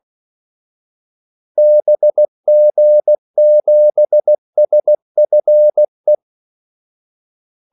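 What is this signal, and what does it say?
Morse code, heard as a keyed tone, about 16 wpm, 600 Hz. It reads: BG7SFE
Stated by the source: Morse call sign practice at 16 wpm.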